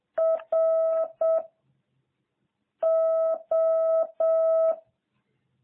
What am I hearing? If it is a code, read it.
Morse code, heard as a keyed tone, about 7 wpm, 660 Hz. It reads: R O